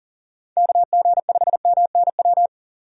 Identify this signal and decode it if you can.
Morse code, heard as a keyed tone, about 38 wpm, 700 Hz. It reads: KG5MNW
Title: USA Call Signs - 6 Characters - 40wpm